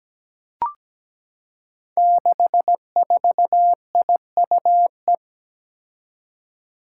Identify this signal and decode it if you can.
Morse code, heard as a keyed tone, about 17 wpm, 700 Hz. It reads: 64IUE
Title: Five Challenging Letter-Number Combinations 17wpm